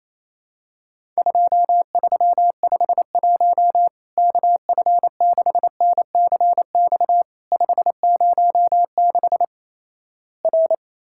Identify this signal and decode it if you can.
Morse code, heard as a keyed tone, about 28 wpm, 700 Hz. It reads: 2351 KF6NCX 506 R
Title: Worked All Europe DX Contest - QTC Exchange - 28wpm